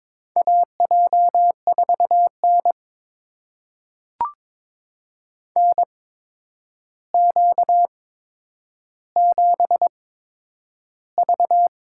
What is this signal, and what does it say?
Morse code, heard as a keyed tone, about 22 wpm, spaced as for 9 wpm, 700 Hz. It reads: AJ4N NQ7V
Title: USA Call Signs - 4 Characters - 8x Character Spacing - 22wpm